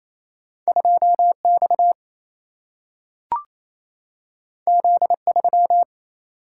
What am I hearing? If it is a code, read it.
Morse code, heard as a keyed tone, about 28 wpm, 700 Hz. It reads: 2 X Z 3
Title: Single-Letter Single-Number 28wpm